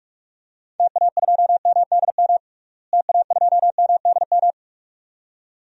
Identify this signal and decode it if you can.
Morse code, heard as a keyed tone, about 46 wpm, 700 Hz. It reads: TA2MDM TA2MDM
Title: DXCC-100 Call Signs - 6 Characters - Repeated 2x - 45wpm